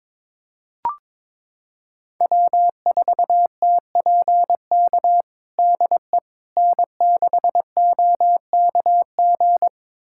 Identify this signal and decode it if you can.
Morse code, heard as a keyed tone, about 22 wpm, 700 Hz. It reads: W4TPK DE N6OKG
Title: USA Callsign DE Callsign - 5 Characters - 22wpm